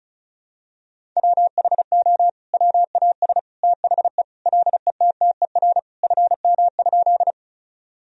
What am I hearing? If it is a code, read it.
Morse code, heard as a keyed tone, about 36 wpm, 700 Hz. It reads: WHO WAS THE LETTER FM?
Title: Sentences from Top 300 Words and Abbreviations 35wpm